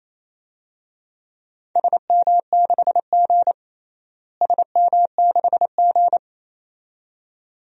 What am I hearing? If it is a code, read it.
Morse code, heard as a keyed tone, about 28 wpm, 700 Hz. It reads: SM6G SM6G
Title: DXCC-100 Call Signs - 4 Characters - Repeated 2x - 28wpm